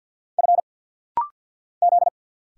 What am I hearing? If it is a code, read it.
Morse code, heard as a keyed tone, about 50 wpm, 700 Hz. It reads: F Z
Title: Single Letter-Number Alt - Rapid-Fire 50wpm